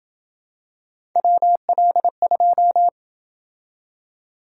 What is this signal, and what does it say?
Morse code, heard as a keyed tone, about 29 wpm, 700 Hz. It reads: WL2